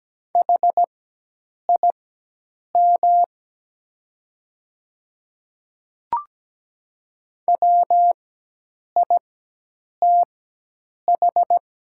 Morse code, heard as a keyed tone, about 17 wpm, spaced as for 8 wpm, 700 Hz. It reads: HIM WITH
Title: 4x Character Spacing - Top 100 Words - 17wpm